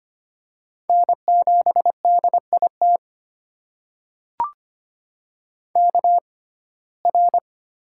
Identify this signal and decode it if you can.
Morse code, heard as a keyed tone, about 25 wpm, spaced as for 18 wpm, 700 Hz. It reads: N7DIT KR
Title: USA Call Signs - 5 Characters - 6x Character Spacing - 25wpm